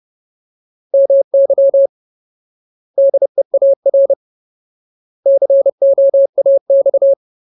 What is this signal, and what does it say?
Morse code, heard as a keyed tone, about 30 wpm, 550 Hz. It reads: MY DEAR COAX
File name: Ultimate Word List / 30WPM Ultimate